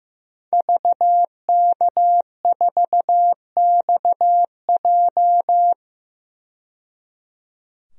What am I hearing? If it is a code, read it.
Morse code, heard as a keyed tone, about 15 wpm, 700 Hz. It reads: VK4XJ
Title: DXCC-100 Call Signs - 5 Characters - 15wpm